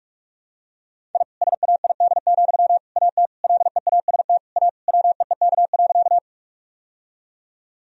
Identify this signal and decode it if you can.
Morse code, heard as a keyed tone, about 41 wpm, 700 Hz. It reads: I SAID, AT LEAST A WEEK.
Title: Sentences from Top 1000 Words - Review Part 2 - 45wpm